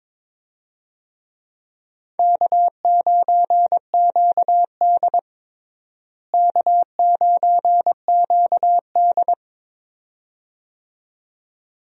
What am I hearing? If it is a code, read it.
Morse code, heard as a keyed tone, about 22 wpm, 700 Hz. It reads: K9QD K9QD
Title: USA Call Signs - 4 Characters - Repeated 2x - 22wpm